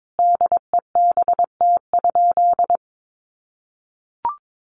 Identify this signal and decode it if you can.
Morse code, heard as a keyed tone, about 22 wpm, 700 Hz. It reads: DEBT?
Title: Sentences from Top 1300 Words 22wpm